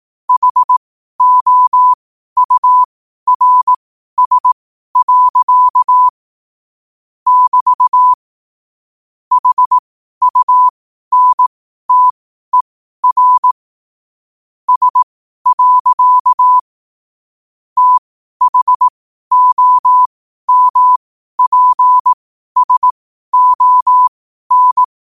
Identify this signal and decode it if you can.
Morse code, heard as a keyed tone, about 18 wpm, spaced as for 12 wpm, 1000 Hz. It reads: HOURS. <BT> HUNTER S. THOMPSON
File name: 12wpm